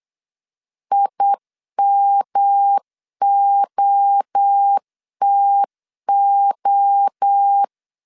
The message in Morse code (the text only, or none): IMOTO